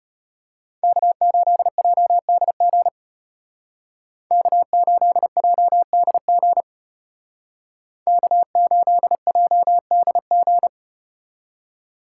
Morse code, heard as a keyed tone, about 34 wpm, 700 Hz. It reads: K8JDG K8JDG K8JDG